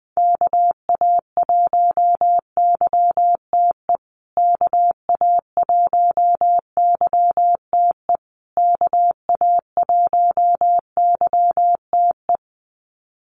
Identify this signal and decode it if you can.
Morse code, heard as a keyed tone, about 20 wpm, 700 Hz. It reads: KA1YTE KA1YTE KA1YTE
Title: USA Call Signs Repeated 3x 20wpm